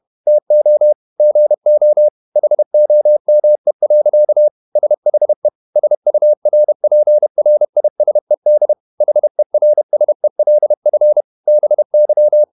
Morse code, heard as a keyed tone, about 31 wpm, 600 Hz. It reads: TO GO HOME. SHE SURPRISED HERSELF BY
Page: Never 31wpm | CW med Gnister